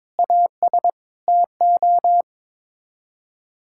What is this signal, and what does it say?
Morse code, heard as a keyed tone, about 22 wpm, 700 Hz. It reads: AS TO
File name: practice / 22wpm / Sets of 6 Words - 22wpm